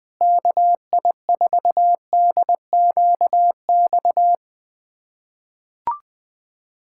Morse code, heard as a keyed tone, about 20 wpm, 700 Hz. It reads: KI4DQX
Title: USA Callsign DE Callsign 20wpm